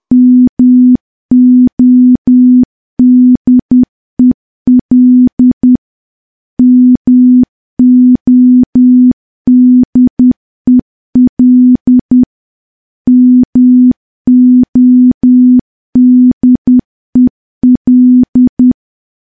Morse code, heard as a keyed tone, about 10 wpm, 250 Hz. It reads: MODEL MODEL MODEL
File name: morse / morse-training-10WPM